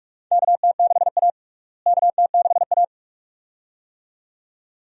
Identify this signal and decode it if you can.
Morse code, heard as a keyed tone, about 42 wpm, 700 Hz. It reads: KT6A KT6A
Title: USA Call Signs - 4 Characters - Repeated 2x - 45wpm